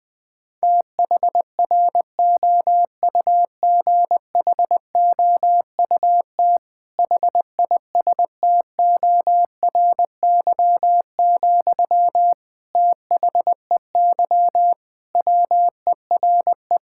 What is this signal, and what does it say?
Morse code, heard as a keyed tone, about 19 wpm, 700 Hz. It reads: THROUGHOUT HISTORY, THEY WERE
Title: Sentences from Top 1000 Words - Review Part 5 - 20wpm